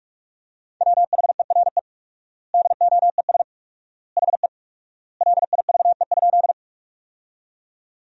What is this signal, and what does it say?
Morse code, heard as a keyed tone, about 41 wpm, 700 Hz. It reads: WHERE DOES HE LIVE?
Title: 4x Word-Spacing - Sentences from Top 200 Words - 45wpm